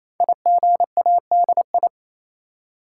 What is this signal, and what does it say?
Morse code, heard as a keyed tone, about 27 wpm, 700 Hz. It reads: IGADI
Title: Code Groups - 5 Letters - 28wpm